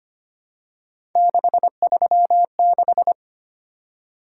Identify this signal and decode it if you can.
Morse code, heard as a keyed tone, about 25 wpm, 700 Hz. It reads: 636
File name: Three Letter-Number Combinations - 25wpm